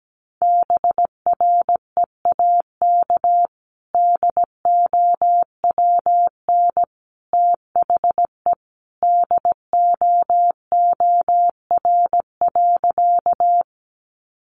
Morse code, heard as a keyed tone, about 17 wpm, 700 Hz. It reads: BREAK DOWN THE DOOR.
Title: Sentences from Top 900 Words 17wpm